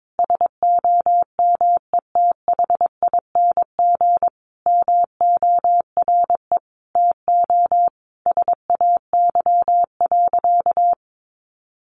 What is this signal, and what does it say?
Morse code, heard as a keyed tone, about 22 wpm, 700 Hz. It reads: SOMETHING MORE TO SAY.
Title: Sentences from Top 600 Words 22wpm